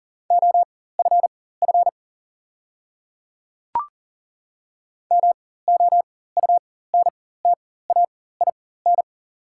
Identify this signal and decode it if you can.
Morse code, heard as a keyed tone, about 40 wpm, spaced as for 19 wpm, 690 Hz. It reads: OFF MOUNTAIN
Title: 4x Character Spacing - Top 300 Words - 40wpm